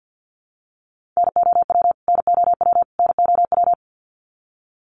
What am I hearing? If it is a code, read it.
Morse code, heard as a keyed tone, about 50 wpm, 700 Hz. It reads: NOW NOW NOW
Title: Top 300 Words Repeated 50wpm